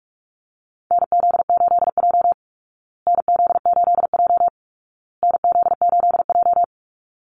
Morse code, heard as a keyed tone, about 44 wpm, 700 Hz. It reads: NZ8J NZ8J NZ8J
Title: Call Signs Repeated 2x - 45wpm